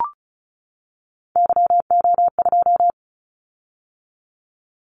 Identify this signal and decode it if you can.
Morse code, heard as a keyed tone, about 36 wpm, 700 Hz. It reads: YO2
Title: Callsign Prefixes for Top 25 DXCC - 35wpm